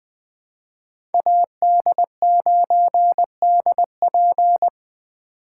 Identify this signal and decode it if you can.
Morse code, heard as a keyed tone, about 21 wpm, 700 Hz. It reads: AD9DP